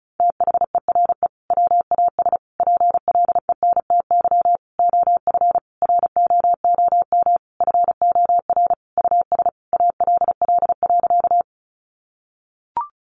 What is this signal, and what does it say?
Morse code, heard as a keyed tone, about 34 wpm, 700 Hz. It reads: THERE WAS PLENTY OF ROOM FOR US ALL.
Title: Sentences from Top 1300 Words 35wpm